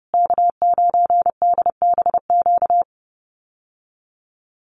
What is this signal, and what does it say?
Morse code, heard as a keyed tone, about 28 wpm, 700 Hz. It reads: K9DBQ
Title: USA Call Signs Repeated 3x 30wpm